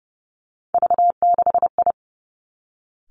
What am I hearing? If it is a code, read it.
Morse code, heard as a keyed tone, about 30 wpm, 700 Hz. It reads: V6I
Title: Three Challenging Letter-Number Combinations 30wpm